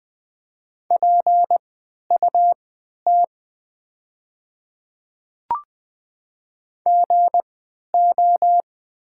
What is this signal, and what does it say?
Morse code, heard as a keyed tone, about 20 wpm, spaced as for 11 wpm, 690 Hz. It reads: PUT GO
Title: 3x Character Spacing - Top 200 Words - 20wpm